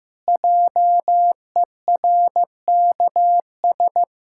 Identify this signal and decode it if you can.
Morse code, heard as a keyed tone, about 16 wpm, 700 Hz. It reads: JERKS
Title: Pangram Challenge v2 - 15wpm